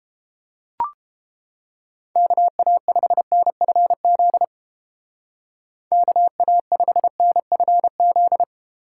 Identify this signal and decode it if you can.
Morse code, heard as a keyed tone, about 31 wpm, 700 Hz. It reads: KA5NFZ KA5NFZ